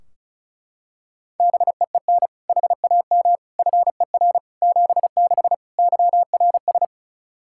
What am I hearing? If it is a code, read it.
Morse code, heard as a keyed tone, about 36 wpm, 700 Hz. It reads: BEEN HAM FER 76 YRS
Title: Practice QSO - Exchange 3 - More Info - 35wpm